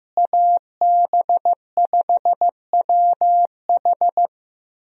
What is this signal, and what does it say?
Morse code, heard as a keyed tone, about 15 wpm, 700 Hz. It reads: AB5WH
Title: USA Call Signs - 5 Characters - Repeated 2x - 15wpm